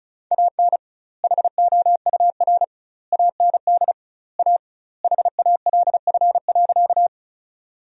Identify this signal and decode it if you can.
Morse code, heard as a keyed tone, about 34 wpm, 700 Hz. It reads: AN HOUR AND A HALF.
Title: 2x Word-Spacing - Sentences from Top 400 Words - 35wpm